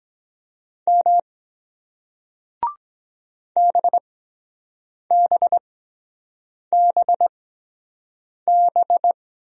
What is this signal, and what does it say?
Morse code, heard as a keyed tone, about 23 wpm, spaced as for 7 wpm, 700 Hz. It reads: M BBBB